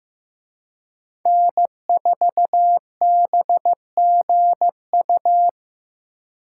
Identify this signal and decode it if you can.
Morse code, heard as a keyed tone, about 18 wpm, 700 Hz. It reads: N4BGU
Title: Speed Racing - USA 1x3 Call Signs 15wpm